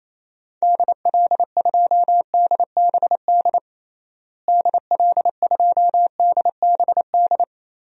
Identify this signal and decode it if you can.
Morse code, heard as a keyed tone, about 28 wpm, 700 Hz. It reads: DL2DBD DL2DBD
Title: DXCC-100 Call Signs - 6 Characters - Repeated 2x - 28wpm